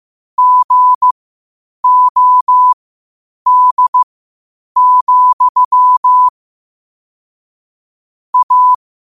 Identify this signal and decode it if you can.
Morse code, heard as a keyed tone, about 15 wpm, spaced as for 8 wpm, 1000 Hz. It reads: GOD, A
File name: qotdfiles / 8wpm